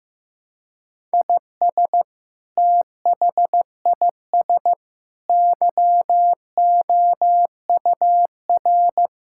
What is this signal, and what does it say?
Morse code, heard as a keyed tone, about 16 wpm, 700 Hz. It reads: IS THIS YOUR
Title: Sentences from Top 1000 Words - Review Part 6 - 15wpm